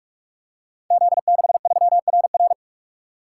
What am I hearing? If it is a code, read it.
Morse code, heard as a keyed tone, about 46 wpm, 700 Hz. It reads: Z63RR
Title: Code Groups - 5 Letter:Numbers - 45wpm